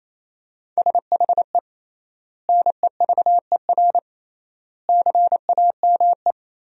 Morse code, heard as a keyed tone, about 28 wpm, 700 Hz. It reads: SHE NEVER CAME
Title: 3x Word-Spacing - Sentences from Top 300 Words - 28wpm